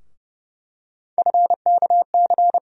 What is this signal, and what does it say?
Morse code, heard as a keyed tone, about 31 wpm, 700 Hz. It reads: FKC
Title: USA Call Sign Postfixes - 3 Letters 30wpm